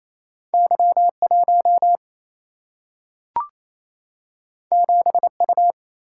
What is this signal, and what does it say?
Morse code, heard as a keyed tone, about 28 wpm, 700 Hz. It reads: Y1 7U
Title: Single-Letter Single-Number 28wpm